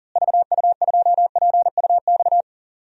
Morse code, heard as a keyed tone, about 40 wpm, 700 Hz. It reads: VU2PUX
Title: DXCC-100 Call Signs - 6 Characters - Repeated 3x - 40wpm